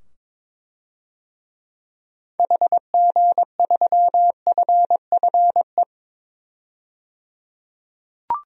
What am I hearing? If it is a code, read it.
Morse code, heard as a keyed tone, about 22 wpm, 700 Hz. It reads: HG3FFE
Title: DXCC-100 Call Signs - 6 Characters - 22wpm